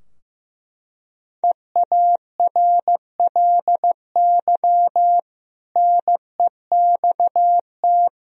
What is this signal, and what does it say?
Morse code, heard as a keyed tone, about 15 wpm, 700 Hz. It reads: EARLY NEXT
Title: Sentences from Top 1000 Words - Review Part 4 - 15wpm